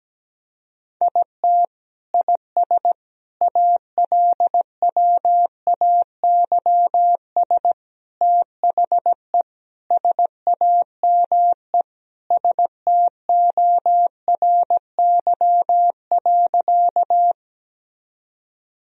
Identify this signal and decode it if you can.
Morse code, heard as a keyed tone, about 17 wpm, 700 Hz. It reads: IT IS ALWAYS THE SAME STORY.
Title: Sentences from Top 1000 Words - Review Part 1 - 17wpm